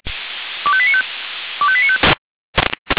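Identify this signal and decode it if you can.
Five-tone selective call (ZVEI), heard as quick runs of five tones, tones of about 70 ms. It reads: 25895 25895
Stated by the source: ZVEI five-tone selective call received on SDR.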